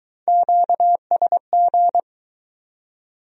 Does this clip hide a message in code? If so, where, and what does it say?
Yes: Morse code from 0.3 to 2.0 s, heard as a keyed tone, about 23 wpm, 700 Hz. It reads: QSG